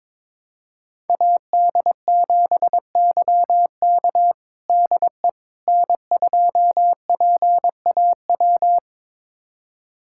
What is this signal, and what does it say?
Morse code, heard as a keyed tone, about 21 wpm, 700 Hz. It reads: AD7YK DE N2PAW
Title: USA Callsign DE Callsign - 5 Characters - 22wpm